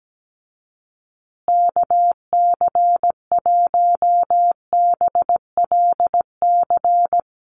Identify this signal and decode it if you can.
Morse code, heard as a keyed tone, about 17 wpm, 700 Hz. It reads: KC1BLC
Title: USA Call Signs Repeated 3x 17wpm